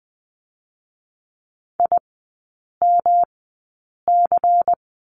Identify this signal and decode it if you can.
Morse code, heard as a keyed tone, about 20 wpm, spaced as for 8 wpm, 700 Hz. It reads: IMC
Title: 166 Lesson 27e - Number 7 - Three Characters 20wpm